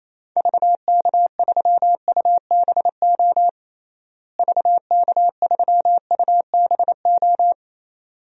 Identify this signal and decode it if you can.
Morse code, heard as a keyed tone, about 28 wpm, 700 Hz. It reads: VK3UBO VK3UBO
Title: DXCC-100 Call Signs - 6 Characters - Repeated 2x - 28wpm